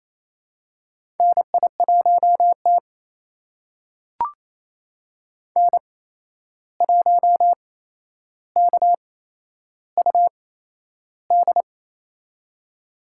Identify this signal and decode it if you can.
Morse code, heard as a keyed tone, about 28 wpm, spaced as for 11 wpm, 700 Hz. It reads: NI1T N1KUD
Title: USA Call Signs - 8x Character Spacing - 28wpm